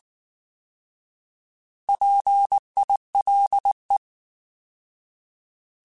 Morse code, heard as a keyed tone, about 19 wpm, 800 Hz. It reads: PILE